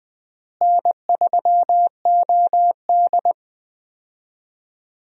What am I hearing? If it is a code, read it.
Morse code, heard as a keyed tone, about 21 wpm, 700 Hz. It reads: N3OD
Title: USA Call Signs - 4 Characters - Repeated 2x - 20wpm